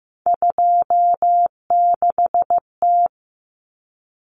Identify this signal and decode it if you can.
Morse code, heard as a keyed tone, about 15 wpm, 700 Hz. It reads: 26T
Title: Common Cut Numbers - 3-Digit 15wpm